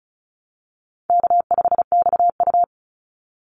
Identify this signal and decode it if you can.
Morse code, heard as a keyed tone, about 34 wpm, 700 Hz. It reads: K5XU
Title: Call Signs - 35wpm